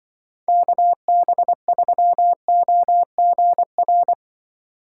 Morse code, heard as a keyed tone, about 24 wpm, 700 Hz. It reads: KB3OGR
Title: Speed Racing - USA 2x3 Call Signs 20wpm